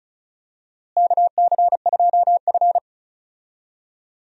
Morse code, heard as a keyed tone, about 36 wpm, 700 Hz. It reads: KC2F
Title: USA Call Signs - 4 Characters - Repeated 2x - 35wpm